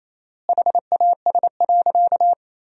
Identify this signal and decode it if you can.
Morse code, heard as a keyed tone, about 28 wpm, 700 Hz. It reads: HAS.
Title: 4x Word-Spacing - Sentences from Top 500 Words - 28wpm